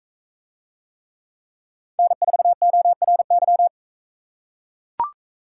Morse code, heard as a keyed tone, about 40 wpm, 700 Hz. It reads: N4ORY